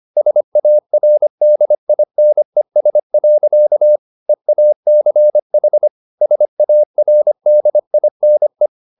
Morse code, heard as a keyed tone, about 25 wpm, 600 Hz. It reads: SARDINES. EACH SARDINE
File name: the old man and the sea part 1 25wpm